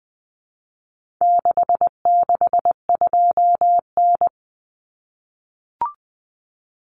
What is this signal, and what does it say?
Morse code, heard as a keyed tone, about 20 wpm, 700 Hz. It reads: 662N
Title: Common Cut Numbers - 4-Digit 20wpm